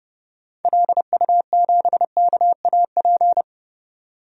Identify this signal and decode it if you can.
Morse code, heard as a keyed tone, about 30 wpm, 700 Hz. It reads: LU7KAP
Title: DXCC-100 Call Signs - 6 Characters - 30wpm